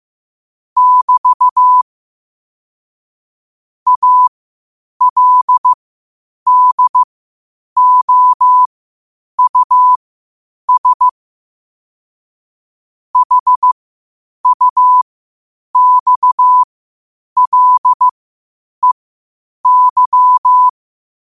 Morse code, heard as a keyed tone, about 15 wpm, spaced as for 8 wpm, 1000 Hz. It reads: <BT> ALDOUS HUXLEY
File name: qotdfiles / 8wpm